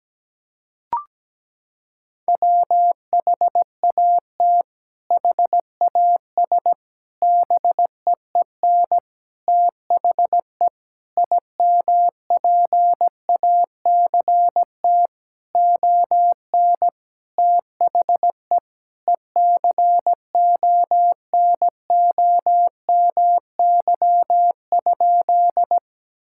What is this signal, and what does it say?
Morse code, heard as a keyed tone, about 17 wpm, 700 Hz. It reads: WHAT HAS BEEN THE IMPACT ON THE ECONOMY?
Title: Sentences from Top 1000 Words - Review Part 6 - 17wpm